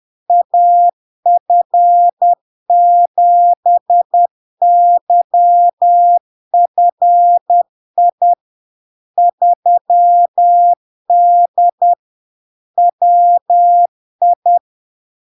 Morse code, heard as a keyed tone, about 10 wpm, 700 Hz. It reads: AF7YFI 3D WI
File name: LARK-FD-2022-Log-in-CW-10wpm-700-Hz